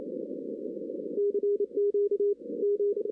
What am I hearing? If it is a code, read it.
Morse code, heard as a keyed tone, about 28 wpm, 410 Hz. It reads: CQ M
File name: 2016 arrl dx cw / 2016-ARRL-DX-CW-K5TR- - 0-9